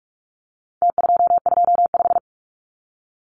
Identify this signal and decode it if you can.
Morse code, heard as a keyed tone, about 44 wpm, 700 Hz. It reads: T225
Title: Numbers - Contest Style - 4-Digit 45wpm